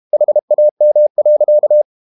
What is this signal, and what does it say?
Morse code, heard as a keyed tone, about 32 wpm, 600 Hz. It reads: HAM.
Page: Ferie 32 wpm